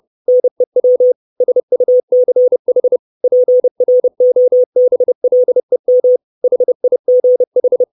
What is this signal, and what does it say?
Morse code, heard as a keyed tone, about 30 wpm, 500 Hz. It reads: NEW SUCH PROBLEM HIGH
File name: 632 Words 30WPM